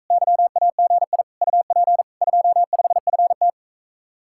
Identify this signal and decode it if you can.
Morse code, heard as a keyed tone, about 45 wpm, 700 Hz. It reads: YAGI UP 25FT